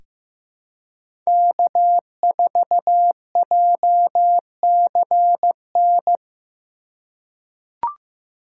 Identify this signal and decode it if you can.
Morse code, heard as a keyed tone, about 15 wpm, 700 Hz. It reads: K4JCN
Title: USA Call Signs - 5 Characters - 2x Character Spacing - 15wpm